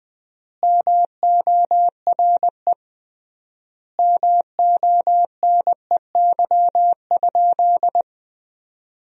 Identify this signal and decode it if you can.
Morse code, heard as a keyed tone, about 20 wpm, 700 Hz. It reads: MORE MONEY?
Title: 3x Word-Spacing - Sentences from Top 400 Words - 20wpm